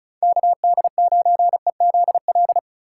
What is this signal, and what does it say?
Morse code, heard as a keyed tone, about 47 wpm, spaced as for 41 wpm, 700 Hz. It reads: KD9EZL
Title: Speed Racing - USA 2x3 Call Signs 35wpm